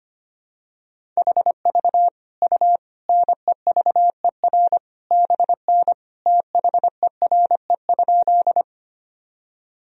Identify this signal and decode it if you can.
Morse code, heard as a keyed tone, about 25 wpm, 700 Hz. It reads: HV U NEVER BN THERE?
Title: Sentences from Top 300 Words and Abbreviations 25wpm